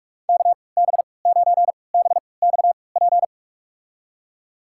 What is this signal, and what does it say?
Morse code, heard as a keyed tone, about 43 wpm, spaced as for 25 wpm, 700 Hz. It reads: KB9BXP
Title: USA Call Signs - 6 Characters - 3x Character Spacing - 45wpm